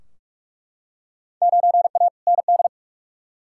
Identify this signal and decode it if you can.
Morse code, heard as a keyed tone, about 46 wpm, 700 Hz. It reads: 9A ND
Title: Field Day Exchanges Encore - 45wpm